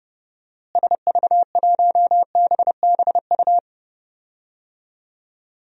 Morse code, heard as a keyed tone, about 32 wpm, 700 Hz. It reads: SV1BBU
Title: DXCC-100 Call Signs - 6 Characters - Repeated 2x - 30wpm